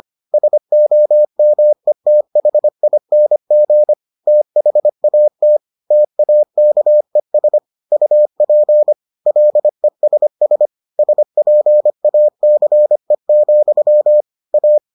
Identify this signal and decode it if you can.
Morse code, heard as a keyed tone, about 25 wpm, 600 Hz. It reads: SOMETHING THAT TAKES UP LESS SPACE, A